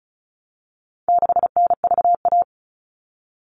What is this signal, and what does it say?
Morse code, heard as a keyed tone, about 34 wpm, 700 Hz. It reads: 6NVA
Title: All Cut Numbers - 4-Digit 35wpm